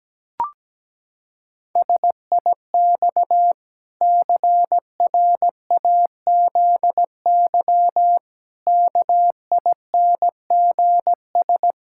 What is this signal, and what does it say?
Morse code, heard as a keyed tone, about 17 wpm, 700 Hz. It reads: SIX CRAZY KINGS
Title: Pangram Challenge v2 - 17wpm